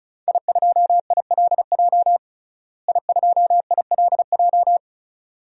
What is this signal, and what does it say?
Morse code, heard as a keyed tone, about 34 wpm, 700 Hz. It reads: I2ILJ I2ILJ
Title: DXCC-100 Call Signs - 5 Characters - Repeated 2x - 35wpm